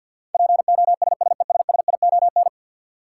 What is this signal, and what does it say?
Morse code, heard as a keyed tone, about 49 wpm, 700 Hz. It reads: POSSESSION
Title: Sentences from Top 1400 Words - New Word Drill - 50wpm